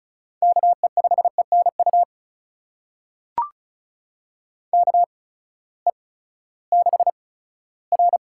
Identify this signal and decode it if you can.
Morse code, heard as a keyed tone, about 36 wpm, spaced as for 18 wpm, 700 Hz. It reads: KE5ENU KE6R
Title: USA Call Signs - 6 Characters - 8x Character Spacing - 35wpm